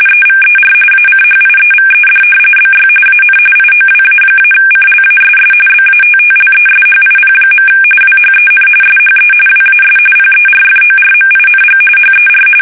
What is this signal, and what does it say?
RTTY, about 45.45 baud. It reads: EVER ACHIEVED BY AN IOWA-CLASS BATTLESHIP. WHEN SHE WAS PERFORMING HER
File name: fsk850 / battleshipnj